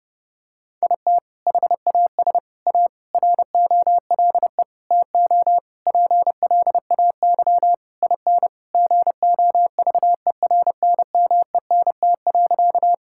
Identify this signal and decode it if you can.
Morse code, heard as a keyed tone, about 30 wpm, 700 Hz. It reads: IT HAS A ROLE TO PLAY IN GOVERNMENT.
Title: Sentences from Top 1000 Words - Review Part 6 - 30wpm